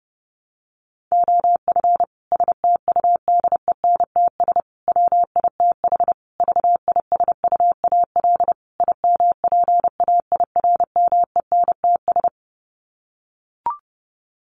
Morse code, heard as a keyed tone, about 30 wpm, 700 Hz. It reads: OF STUDENTS WITH VISUAL IMPAIRMENTS
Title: Sets of 5 Words Encore 30wpm